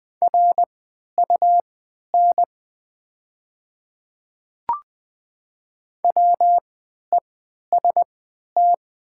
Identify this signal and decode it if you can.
Morse code, heard as a keyed tone, about 20 wpm, spaced as for 11 wpm, 700 Hz. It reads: RUN WEST
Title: 3x Character Spacing - Top 500 Words - 20wpm